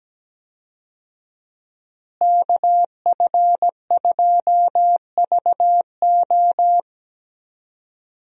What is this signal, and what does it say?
Morse code, heard as a keyed tone, about 17 wpm, 700 Hz. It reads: KF2VO
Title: USA Call Signs - 5 Characters - Repeated 3x - 17wpm